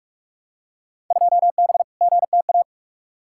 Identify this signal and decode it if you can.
Morse code, heard as a keyed tone, about 46 wpm, 700 Hz. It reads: 2B GTA
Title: Field Day Exchanges Encore - 45wpm